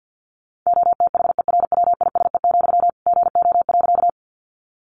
Morse code, heard as a keyed tone, about 50 wpm, 700 Hz. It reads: OTHERWISE, GO.
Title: Sentences from Top 1200 Words 50wpm